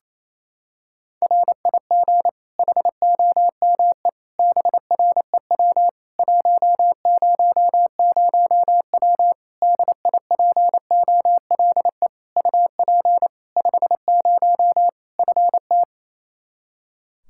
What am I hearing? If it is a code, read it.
Morse code, heard as a keyed tone, about 28 wpm, 700 Hz. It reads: RIG HOME BREW 100W DIPOLE UP 50 FT
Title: Practice QSO - Exchange 2 - Info - 28wpm